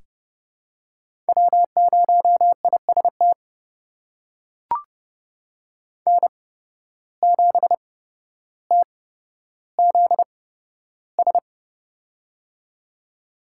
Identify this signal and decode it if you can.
Morse code, heard as a keyed tone, about 31 wpm, spaced as for 13 wpm, 700 Hz. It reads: W0IST N7TZS
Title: USA Call Signs - 5 Characters - 8x Character Spacing - 30wpm